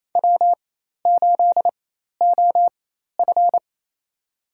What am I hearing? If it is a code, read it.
Morse code, heard as a keyed tone, about 28 wpm, spaced as for 13 wpm, 700 Hz. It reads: W8OF
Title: USA Call Signs - 4 Characters - 4x Character Spacing - 28wpm